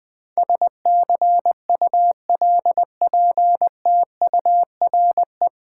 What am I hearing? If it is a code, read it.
Morse code, heard as a keyed tone, about 20 wpm, 700 Hz. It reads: SCULPTURE
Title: Nine Letter Words - 20wpm